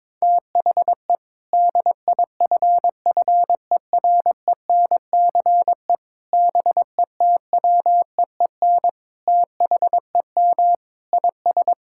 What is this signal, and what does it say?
Morse code, heard as a keyed tone, about 22 wpm, 700 Hz. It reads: THE DIFFERENCE BETWEEN THEM IS
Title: Sentences from Top 1600 Words 22wpm